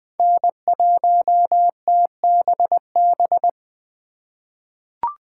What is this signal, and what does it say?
Morse code, heard as a keyed tone, about 20 wpm, 700 Hz. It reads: N1TBB
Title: USA Call Signs - 5 Characters - 4x Character Spacing - 20wpm